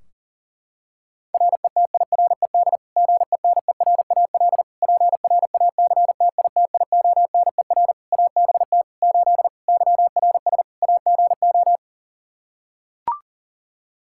Morse code, heard as a keyed tone, about 37 wpm, 700 Hz. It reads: RETIRED GENERAL PRACTITIONER ABT 8 YRS AGO
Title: Practice QSO - Exchange 3 - More Info - 40wpm